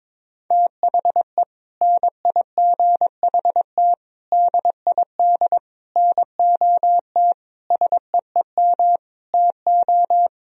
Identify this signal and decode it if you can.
Morse code, heard as a keyed tone, about 22 wpm, 700 Hz. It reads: THE NIGHT DID NOT SEEM TO
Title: Sentences from Top 1000 Words - Review Part 1 - 22wpm